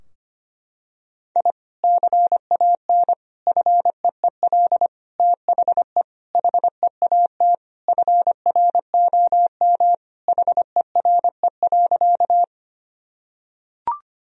Morse code, heard as a keyed tone, about 25 wpm, 700 Hz. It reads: I CAN FEEL THE HEAT FROM HERE.
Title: Sentences from Top 1000 Words - Review Part 3 - 25wpm